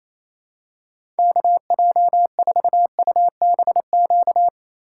Morse code, heard as a keyed tone, about 28 wpm, 700 Hz. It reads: KJ4UBQ
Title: USA Call Signs - 2x Character Spacing - 28wpm